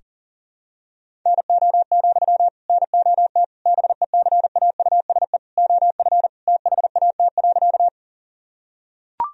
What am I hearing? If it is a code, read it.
Morse code, heard as a keyed tone, about 37 wpm, 700 Hz. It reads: NO, NOT BECAUSE OF THAT.
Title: Sentences from Top 1000 Words - Review Part 3 - 40wpm